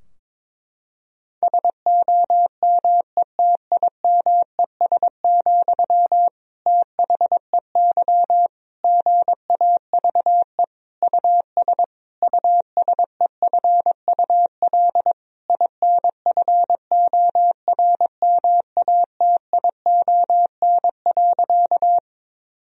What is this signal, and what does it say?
Morse code, heard as a keyed tone, about 21 wpm, 700 Hz. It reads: SOMETIMES, THEY GAVE US USEFUL INFORMATION.
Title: Sentences from Top 1000 Words - Review Part 4 - 22wpm